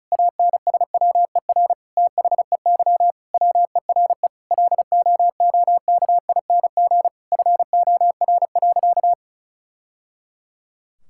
Morse code, heard as a keyed tone, about 36 wpm, 700 Hz. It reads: ANSWER THEY WERE LOOKING FOR.
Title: Sentences from Top 1000 Words - Review Part 6 - 35wpm